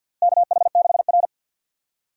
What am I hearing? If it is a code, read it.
Morse code, heard as a keyed tone, about 52 wpm, 700 Hz. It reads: KH6R
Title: DXCC-100 Call Signs - 4 Characters - Repeated 2x - 50wpm